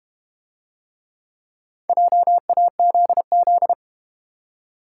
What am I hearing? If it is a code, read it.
Morse code, heard as a keyed tone, about 31 wpm, 700 Hz. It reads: JAZZ
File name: Easy Words - Speed-Racing with Hints - 25wpm